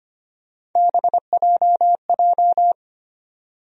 Morse code, heard as a keyed tone, about 25 wpm, 700 Hz. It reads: BJJ
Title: USA Call Sign Postfixes - 3 Letters 25wpm